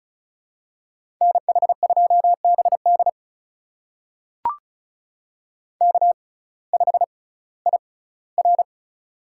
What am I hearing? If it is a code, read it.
Morse code, heard as a keyed tone, about 36 wpm, spaced as for 19 wpm, 700 Hz. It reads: NH2BD K5IR
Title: USA Call Signs - 6x Character Spacing - 35wpm